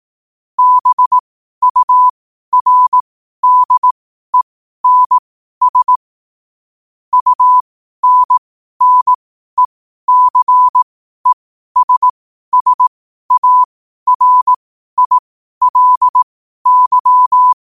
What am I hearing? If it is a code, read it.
Morse code, heard as a keyed tone, about 18 wpm, spaced as for 12 wpm, 1000 Hz. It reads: BURDENS UNNECESSARILY
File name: 12wpm